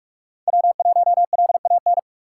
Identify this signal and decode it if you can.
Morse code, heard as a keyed tone, about 46 wpm, 700 Hz. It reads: W1LAN
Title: USA Call Signs - 5 Characters - 3x Character Spacing - 45wpm